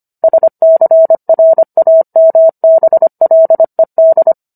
Morse code, heard as a keyed tone, about 25 wpm, 650 Hz. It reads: SCRAMBLED